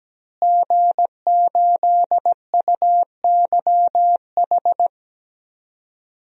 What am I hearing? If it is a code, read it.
Morse code, heard as a keyed tone, about 17 wpm, 700 Hz. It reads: G8UYH
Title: DXCC-100 Call Signs - 5 Characters - Repeated 3x - 17wpm